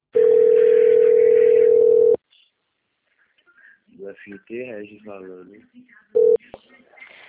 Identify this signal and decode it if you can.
Ringback tone (North America), 440 + 480 Hz, with speech.
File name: out- - - - - .6568